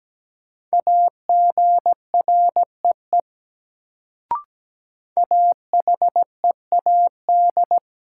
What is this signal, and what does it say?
Morse code, heard as a keyed tone, about 17 wpm, 700 Hz. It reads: AGREE AHEAD
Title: Sentences from Top 1000 Words - Review Part 6 - New Word Drill - 17wpm